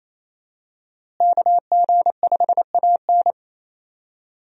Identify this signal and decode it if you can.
Morse code, heard as a keyed tone, about 28 wpm, 700 Hz. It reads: KG5AN
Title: USA Call Signs - 5 Characters - Repeated 2x - 28wpm